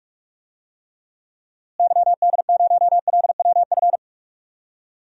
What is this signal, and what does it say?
Morse code, heard as a keyed tone, about 41 wpm, 700 Hz. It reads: YD0LWF
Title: DXCC-100 Call Signs - 6 Characters - Repeated 3x - 45wpm